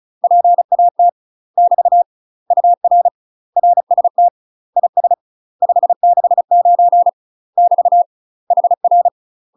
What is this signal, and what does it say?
Morse code, heard as a keyed tone, about 34 wpm, 700 Hz. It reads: PAT <BT> UR RST IS 569 <BT> HR
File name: UK MASTER QSOs TRAINING/35WPM